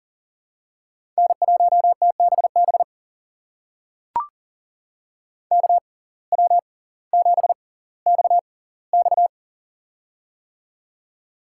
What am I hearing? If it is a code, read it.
Morse code, heard as a keyed tone, about 39 wpm, spaced as for 20 wpm, 700 Hz. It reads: N1TBB KW7XX
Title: USA Call Signs - 5 Characters - 6x Character Spacing - 40wpm